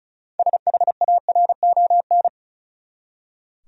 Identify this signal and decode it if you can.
Morse code, heard as a keyed tone, about 36 wpm, 700 Hz. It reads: SHARON
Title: USA Common Names - 35wpm